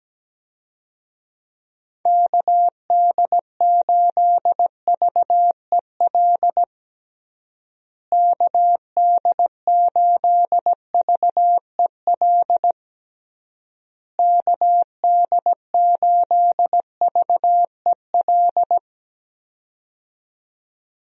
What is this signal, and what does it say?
Morse code, heard as a keyed tone, about 17 wpm, 700 Hz. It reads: KD8VEL KD8VEL KD8VEL
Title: USA Call Signs - 6 Characters - Repeated 3x - 17wpm